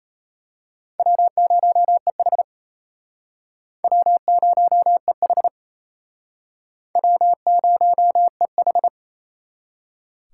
Morse code, heard as a keyed tone, about 32 wpm, 700 Hz. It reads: W0EH W0EH W0EH
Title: Speed Racing - USA 1x2 Call Signs 28wpm